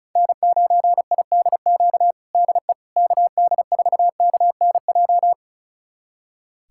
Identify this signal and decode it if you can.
Morse code, heard as a keyed tone, about 36 wpm, 700 Hz. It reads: N9IDQ DE KD4KNJ
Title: USA Callsign DE Callsign 35wpm